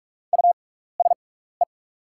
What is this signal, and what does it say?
Morse code, heard as a keyed tone, about 44 wpm, spaced as for 15 wpm, 700 Hz. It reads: USE
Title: 6x Character Spacing - Top 100 Words - 45wpm